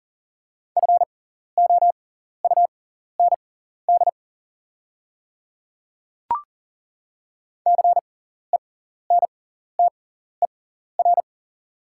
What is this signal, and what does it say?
Morse code, heard as a keyed tone, about 40 wpm, spaced as for 14 wpm, 690 Hz. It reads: FOUND CENTER
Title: 6x Character Spacing - Top 500 Words - 40wpm